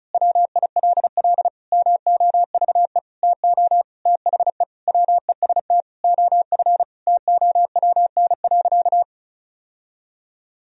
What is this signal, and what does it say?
Morse code, heard as a keyed tone, about 36 wpm, 700 Hz. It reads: WILL MOVE TO THE WEST OF TOWN.
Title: Sentences from Top 1000 Words - Review Part 2 - 35wpm